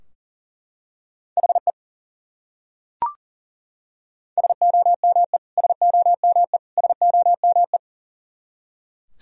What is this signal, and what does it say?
Morse code, heard as a keyed tone, about 40 wpm, 700 Hz. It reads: HE SOME SOME SOME
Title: Top 100 Words Repeated 40wpm